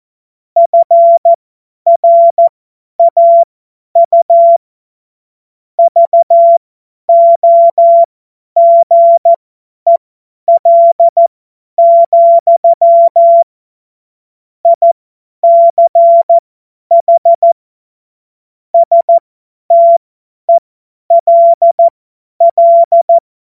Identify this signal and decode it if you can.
Morse code, heard as a keyed tone, about 14 wpm, spaced as for 10 wpm, 680 Hz. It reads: FRAU VOGEL, ICH STELL
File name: Morsekurs / Witze / Witz 04. Juli 2023 50 F70